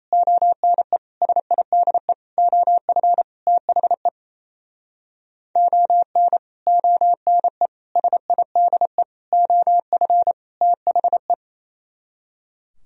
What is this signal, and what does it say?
Morse code, heard as a keyed tone, about 37 wpm, 700 Hz. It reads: ONE SIDE OF THE ON ONE SIDE OF THE